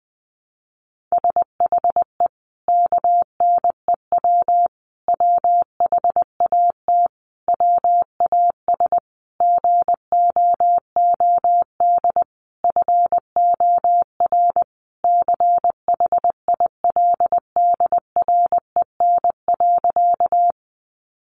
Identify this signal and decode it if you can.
Morse code, heard as a keyed tone, about 20 wpm, 700 Hz. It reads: SHE KNEW WHAT WAS GOOD FOR CHILDREN.
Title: Sentences from Top 400 Words 20wpm